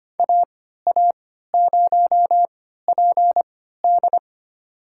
Morse code, heard as a keyed tone, about 25 wpm, spaced as for 14 wpm, 700 Hz. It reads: AA0PD
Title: USA Call Signs - 5 Characters - 3x Character Spacing - 25wpm